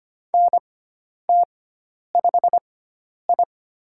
Morse code, heard as a keyed tone, about 25 wpm, spaced as for 10 wpm, 700 Hz. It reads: NT5I
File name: Lesson 16g - Letter c - Call Signs 25wpm